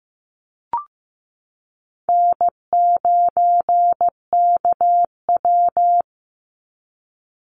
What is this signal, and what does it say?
Morse code, heard as a keyed tone, about 14 wpm, 700 Hz. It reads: N9KW
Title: NA SOTA Chaser Callsigns 15wpm